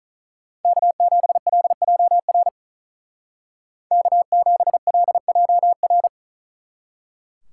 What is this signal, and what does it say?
Morse code, heard as a keyed tone, about 47 wpm, 700 Hz. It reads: K7LJR K7LJR